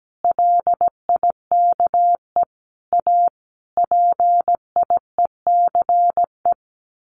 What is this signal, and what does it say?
Morse code, heard as a keyed tone, about 17 wpm, 700 Hz. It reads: LIKE A PIECE
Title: Sentences from Top 1100 Words 17wpm